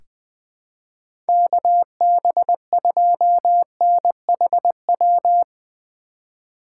Morse code, heard as a keyed tone, about 21 wpm, 700 Hz. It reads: KB2NHW
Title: USA Call Signs - 6 Characters - 3x Character Spacing - 20wpm